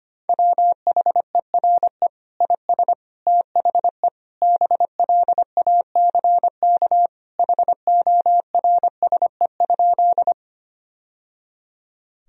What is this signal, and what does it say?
Morse code, heard as a keyed tone, about 25 wpm, 700 Hz. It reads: WHERE IS THE BLACK HORSE?
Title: Sentences from Top 1000 Words - Review Part 2 - 25wpm